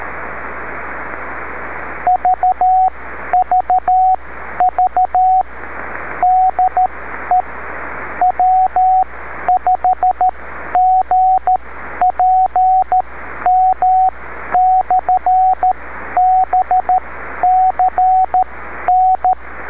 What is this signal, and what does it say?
Morse code, heard as a keyed tone, about 13 wpm, spaced as for 11 wpm, 740 Hz, with radio noise. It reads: VVV DE W5GPM/BCN
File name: beacons / w5gpm-b